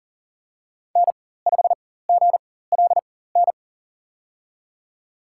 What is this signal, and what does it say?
Morse code, heard as a keyed tone, about 40 wpm, spaced as for 19 wpm, 700 Hz. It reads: N5GLN
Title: USA Call Signs - 5 Characters - 4x Character Spacing - 40wpm